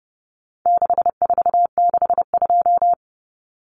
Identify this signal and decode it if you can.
Morse code, heard as a keyed tone, about 29 wpm, 700 Hz. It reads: 6462
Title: Numbers - 4-Digit 30wpm